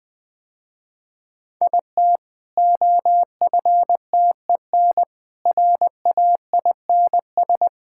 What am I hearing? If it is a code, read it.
Morse code, heard as a keyed tone, about 20 wpm, 700 Hz. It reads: IT OFTEN RAINS
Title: Easy Sentences with Hints - 20wpm